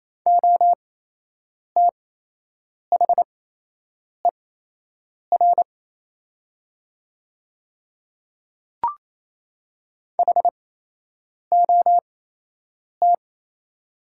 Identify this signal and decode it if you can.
Morse code, heard as a keyed tone, about 27 wpm, spaced as for 8 wpm, 700 Hz. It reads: OTHER HOT
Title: 8x Character Spacing - Top 100 Words - 28wpm